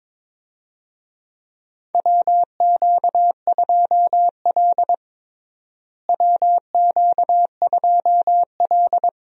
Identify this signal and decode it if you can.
Morse code, heard as a keyed tone, about 22 wpm, 700 Hz. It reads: WQ2L WQ2L
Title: USA Call Signs - 4 Characters - Repeated 2x - 22wpm